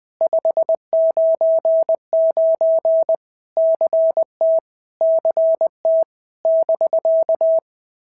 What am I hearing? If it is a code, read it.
Morse code, heard as a keyed tone, about 20 wpm, 650 Hz. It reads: 599 CT CT <BK>
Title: POTA QSO Practice - 20wpm